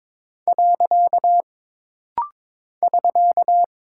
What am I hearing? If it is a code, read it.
Morse code, heard as a keyed tone, about 23 wpm, 700 Hz. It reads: . <SK>
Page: This rapid-fire timing is geared to push your ICR (Instant Character Recognition) skills.